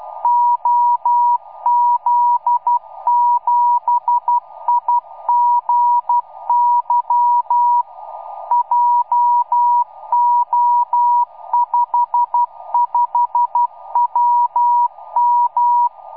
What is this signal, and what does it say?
Morse code, heard as a keyed tone, about 12 wpm, 960 Hz. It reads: OZ7IGY JO55WM
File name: iw0ffk / 2006 06 07 OZ7IGY beacon